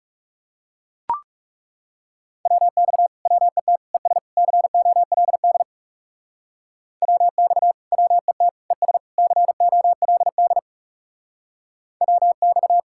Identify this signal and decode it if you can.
Morse code, heard as a keyed tone, about 36 wpm, 700 Hz. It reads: WX WET ES COLD WX WET ES COLD WX
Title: Speed-Racing Common QSO Phrases - 30wpm